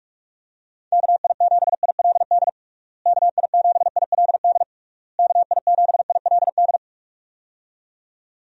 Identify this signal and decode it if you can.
Morse code, heard as a keyed tone, about 46 wpm, 700 Hz. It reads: KI7ILD KI7ILD KI7ILD
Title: USA Call Signs - 6 Characters - Repeated 3x - 45wpm